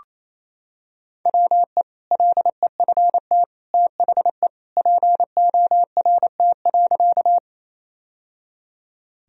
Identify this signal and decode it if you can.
Morse code, heard as a keyed tone, about 28 wpm, 700 Hz. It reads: WE LEFT THE PORT.
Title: Sentences from Top 1000 Words - Review Part 1 - 28wpm